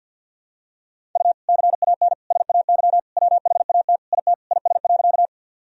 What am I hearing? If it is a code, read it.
Morse code, heard as a keyed tone, about 52 wpm, 700 Hz. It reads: U CAN SAY WHAT IT IS.
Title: Sentences from Top 200 Words and Abbreviations 50wpm